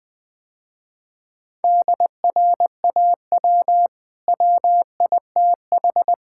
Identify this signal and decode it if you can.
Morse code, heard as a keyed tone, about 20 wpm, 700 Hz. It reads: DRAW WITH
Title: Easy Sentences with Hints - 20wpm